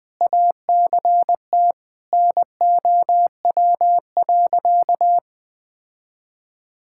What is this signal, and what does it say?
Morse code, heard as a keyed tone, about 20 wpm, 700 Hz. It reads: ACT NOW.
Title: Sentences with 6-Word Phrases - 20wpm